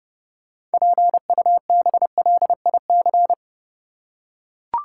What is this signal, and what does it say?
Morse code, heard as a keyed tone, about 30 wpm, 700 Hz. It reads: PUBLIC
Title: Six Letter Words - 30wpm